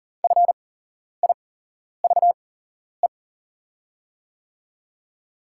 Morse code, heard as a keyed tone, about 41 wpm, spaced as for 11 wpm, 690 Hz. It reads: FIVE
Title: 8x Character Spacing - Top 500 Words - 40wpm